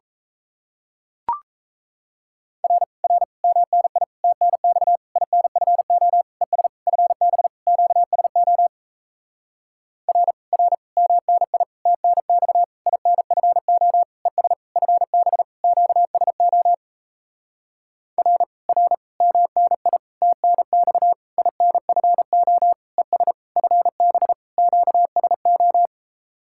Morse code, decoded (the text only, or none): R R MNI TNX INFO ES FB QSO R R MNI TNX INFO ES FB QSO R R MNI TNX INFO ES FB QSO